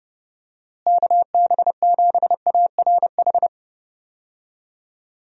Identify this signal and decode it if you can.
Morse code, heard as a keyed tone, about 30 wpm, 700 Hz. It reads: KB7ARH
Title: USA Call Signs - 6 Characters - Repeated 3x - 30wpm